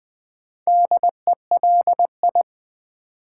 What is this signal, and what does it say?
Morse code, heard as a keyed tone, about 20 wpm, 700 Hz. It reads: DELI
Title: Build A Word - 4 Letters At A Time - 12-Letter Words - 20wpm